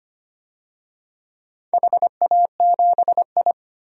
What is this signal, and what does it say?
Morse code, heard as a keyed tone, about 25 wpm, 700 Hz. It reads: HA7I
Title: DXCC-100 Call Signs - 4 Characters - 25wpm